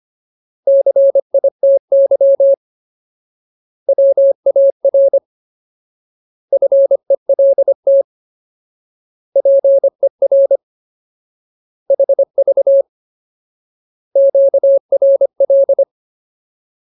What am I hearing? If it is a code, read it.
Morse code, heard as a keyed tone, about 25 wpm, 550 Hz. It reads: CITY WAR FELT PER HV QRL
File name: Ultimate Word List / 25WPM Ultimate